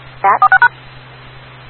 DTMF, heard as telephone touch tones, tones of about 90 ms. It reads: 46#
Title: Rocky Mountain Radio League Repeaters